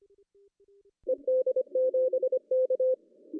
Morse code, heard as a keyed tone, about 25 wpm, 520 Hz. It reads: ED7K
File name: CW / cqww-cw-k5tr